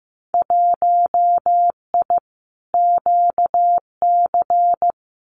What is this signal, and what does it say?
Morse code, heard as a keyed tone, about 15 wpm, 700 Hz. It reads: 1I QC
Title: Winter Field Day Exchanges 15wpm